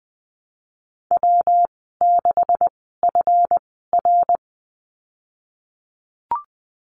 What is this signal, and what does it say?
Morse code, heard as a keyed tone, about 20 wpm, spaced as for 14 wpm, 700 Hz. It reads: W6FR
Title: 173 Lesson 28g - Number 9 - Call Signs 20wpm